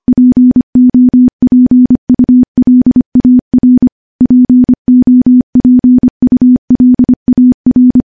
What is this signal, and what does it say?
Morse code, heard as a keyed tone, about 25 wpm, 250 Hz. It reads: POPULAR POPULAR
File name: morse / morse-training-25WPM